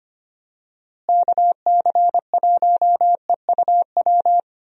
Morse code, heard as a keyed tone, about 25 wpm, 700 Hz. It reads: KC1EUW
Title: USA Call Signs - 6 Characters - Repeated 3x - 25wpm